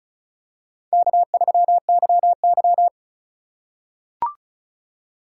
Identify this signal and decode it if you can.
Morse code, heard as a keyed tone, about 36 wpm, 700 Hz. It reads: K3YY
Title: USA Call Signs - 4 Characters - 3x Character Spacing - 35wpm